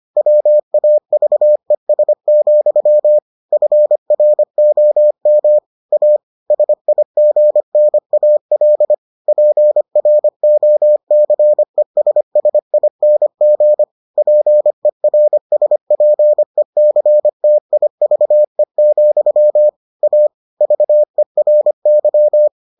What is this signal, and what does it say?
Morse code, heard as a keyed tone, about 25 wpm, 600 Hz. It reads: WAVES, FROM A SIGNAL PROCESSING PERSPECTIVE, A VERY